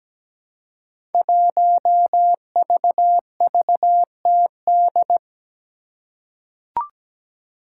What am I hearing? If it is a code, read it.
Morse code, heard as a keyed tone, about 17 wpm, 700 Hz. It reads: 1VVTD
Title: Five Challenging Letter-Number Combinations 17wpm